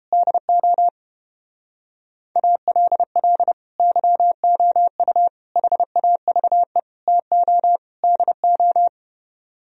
Morse code, heard as a keyed tone, about 29 wpm, 700 Hz. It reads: DO ALL YOU HAVE TO DO